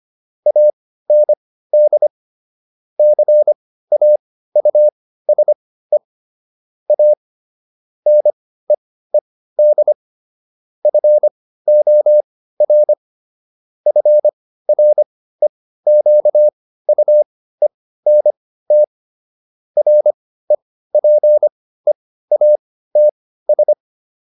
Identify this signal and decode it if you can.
Morse code, heard as a keyed tone, about 25 wpm, spaced as for 15 wpm, 600 Hz. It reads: AND CAUSE A NEED FOR FREQUENT REPEATS